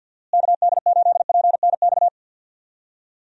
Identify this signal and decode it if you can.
Morse code, heard as a keyed tone, about 51 wpm, 700 Hz. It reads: KD8PNX
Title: USA Call Signs - 6 Characters - Repeated 3x - 50wpm